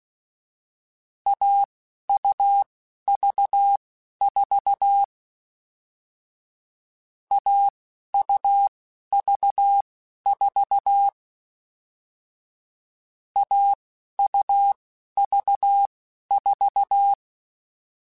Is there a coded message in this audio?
Yes: Morse code from 1.3 to 17.1 s, heard as a keyed tone, about 16 wpm, spaced as for 11 wpm, 800 Hz. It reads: AUV4 AUV4 AUV4